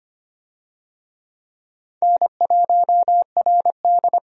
Morse code, heard as a keyed tone, about 25 wpm, 700 Hz. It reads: N1RD
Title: USA Call Signs - 4 Characters - 25wpm